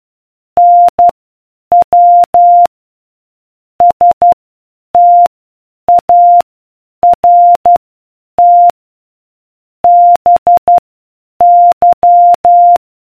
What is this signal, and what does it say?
Morse code, heard as a keyed tone, about 12 wpm, spaced as for 8 wpm, 700 Hz. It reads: NW START BY